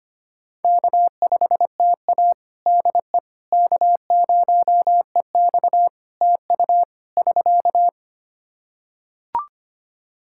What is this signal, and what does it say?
Morse code, heard as a keyed tone, about 25 wpm, 700 Hz. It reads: K5TA DE K0EX TU <SK>
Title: Practice QSO - Exchange 4 - Ending - 25wpm